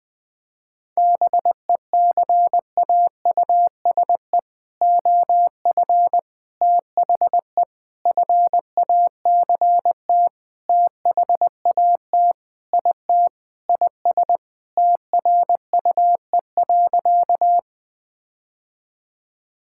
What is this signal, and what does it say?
Morse code, heard as a keyed tone, about 20 wpm, 700 Hz. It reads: BECAUSE OF THE FACT THAT IT IS TRUE.
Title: Sentences with 4-Word Phrases - 20wpm